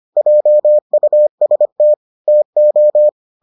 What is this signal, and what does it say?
Morse code, heard as a keyed tone, about 25 wpm, 600 Hz. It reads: JUST TO